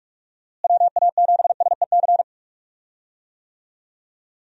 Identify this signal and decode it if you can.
Morse code, heard as a keyed tone, about 41 wpm, 700 Hz. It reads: WA7SEC
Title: USA Call Signs - 6 Characters - Repeated 3x - 45wpm